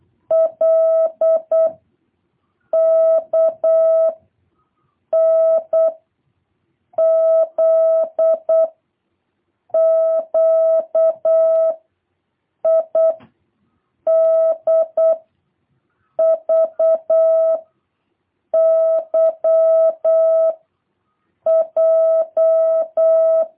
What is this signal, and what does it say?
Morse code, heard as a keyed tone, about 8 wpm, 650 Hz. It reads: L K N Z Q I D V Y J